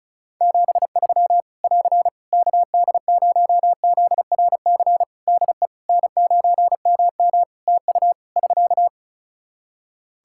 Morse code, heard as a keyed tone, about 34 wpm, 700 Hz. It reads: 73 <AR> KD0ZRC DE N9MM TU <SK>
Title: Practice QSO - Exchange 4 - Ending - 35wpm